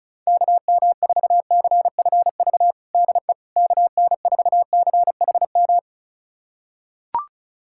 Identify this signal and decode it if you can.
Morse code, heard as a keyed tone, about 34 wpm, 700 Hz. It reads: KM4CFV DE KN4CHM
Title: USA Callsign DE Callsign - 6 Characters - 35wpm